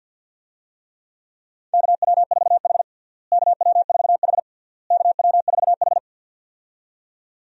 Morse code, heard as a keyed tone, about 52 wpm, 700 Hz. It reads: KW4H KW4H KW4H
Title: USA Call Signs - 4 Characters - Repeated 3x - 50wpm